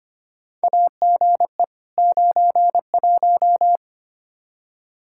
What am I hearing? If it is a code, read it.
Morse code, heard as a keyed tone, about 25 wpm, 700 Hz. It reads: AGE 91
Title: Practice QSO - Exchange 3 - More Info - 25wpm